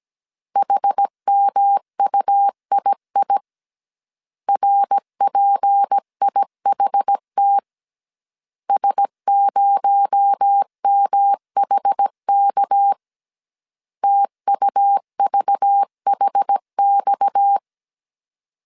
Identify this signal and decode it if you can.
Morse code, heard as a keyed tone, about 17 wpm, 800 Hz. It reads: HMUII RPIHT S0MHK TUVHX
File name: CW / DET / 1 6 1500C S A 16wpm 2635